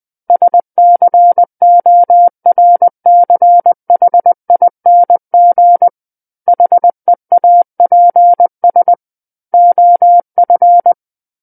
Morse code, heard as a keyed tone, about 20 wpm, 700 Hz. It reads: SCORCHING HEAPS OF
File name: 20 WPM